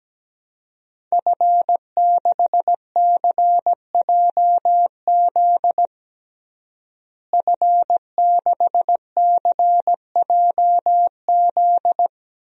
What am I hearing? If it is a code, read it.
Morse code, heard as a keyed tone, about 17 wpm, 700 Hz. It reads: F6CJZ F6CJZ
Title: DXCC-100 Call Signs - 5 Characters - Repeated 3x - 17wpm